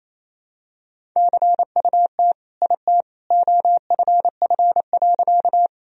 Morse code, heard as a keyed tone, about 28 wpm, 700 Hz. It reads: CUT IT OFF.
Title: Sentences from Top 1000 Words - Review Part 1 - 28wpm